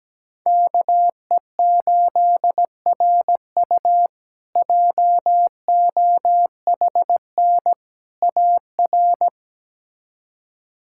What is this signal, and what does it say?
Morse code, heard as a keyed tone, about 17 wpm, 700 Hz. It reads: KE8RU JOHN AR
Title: K1USN SST QSO Practice - 17wpm